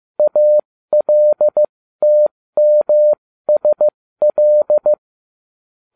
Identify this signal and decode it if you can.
Morse code, heard as a keyed tone, about 15 wpm, spaced as for 13 wpm, 600 Hz. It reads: ALTMSL
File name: cw / ubacw08 2